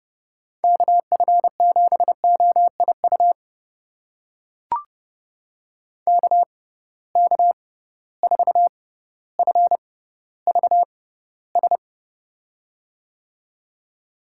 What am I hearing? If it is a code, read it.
Morse code, heard as a keyed tone, about 30 wpm, spaced as for 15 wpm, 700 Hz. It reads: KF7OIU KK4FVS
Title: USA Call Signs - 6 Characters - 6x Character Spacing - 30wpm